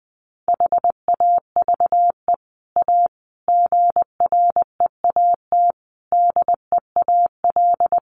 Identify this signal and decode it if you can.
Morse code, heard as a keyed tone, about 20 wpm, 700 Hz. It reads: HAVE A GREAT DEAL
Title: Sets of 5 Words Encore2 20wpm